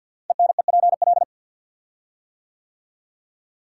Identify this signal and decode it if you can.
Morse code, heard as a keyed tone, about 48 wpm, 700 Hz. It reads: ENEPL
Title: Code Groups - 5 Letters - 50wpm